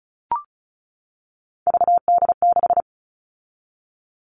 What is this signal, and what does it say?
Morse code, heard as a keyed tone, about 34 wpm, 700 Hz. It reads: VD6
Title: All Cut Numbers - 3-Digit 35wpm